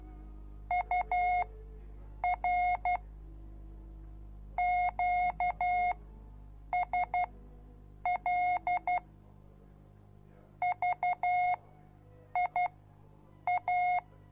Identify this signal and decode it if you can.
Morse code, heard as a keyed tone, about 12 wpm, spaced as for 7 wpm, 720 Hz. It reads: UR QSL VIA